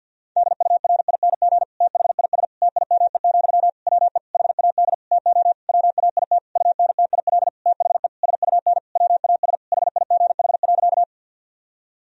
Morse code, heard as a keyed tone, about 52 wpm, 700 Hz. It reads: DURING THIS TIME, WE HAD TO WAIT UNTIL THE SUN WAS HIGH.
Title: Sentences from Top 1000 Words - Review Part 2 - 50wpm